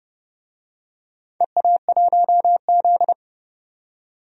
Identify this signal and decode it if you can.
Morse code, heard as a keyed tone, about 29 wpm, 700 Hz. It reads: EA1Z
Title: DXCC-100 Call Signs - 4 Characters - 30wpm